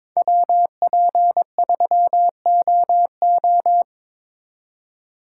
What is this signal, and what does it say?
Morse code, heard as a keyed tone, about 26 wpm, 700 Hz. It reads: WP3OO
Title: Speed Racing - USA 2x2 Call Signs 22wpm